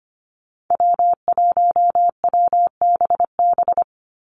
Morse code, heard as a keyed tone, about 25 wpm, 700 Hz. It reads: W1WBB
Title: Call Signs Repeated 2x - 25wpm